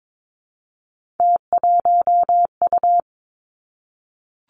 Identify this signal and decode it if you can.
Morse code, heard as a keyed tone, about 22 wpm, 700 Hz. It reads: T1U
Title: Three Challenging Letter-Number Combinations 22wpm